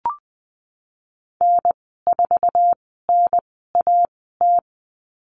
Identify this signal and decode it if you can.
Morse code, heard as a keyed tone, about 20 wpm, spaced as for 15 wpm, 700 Hz. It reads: N4NAT
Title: 062 Lesson 11g - Letter h - Call Signs 20wpm